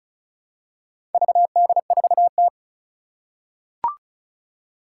Morse code, heard as a keyed tone, about 36 wpm, 700 Hz. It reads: VD4T
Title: Four Challenging Letter-Number Combinations 35wpm